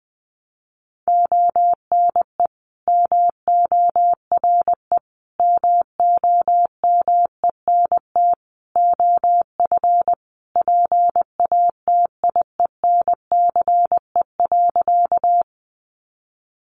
Morse code, heard as a keyed tone, about 20 wpm, 700 Hz. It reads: ONE MORE MOMENT OF PATIENCE.
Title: Sentences from Top 1300 Words 20wpm